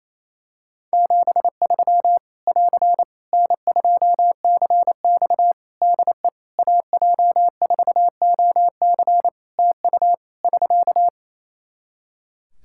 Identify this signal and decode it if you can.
Morse code, heard as a keyed tone, about 28 wpm, 700 Hz. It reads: 73 <AR> N2CX DE AJ4OC TU <SK>
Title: Practice QSO - Exchange 4 - Ending - 28wpm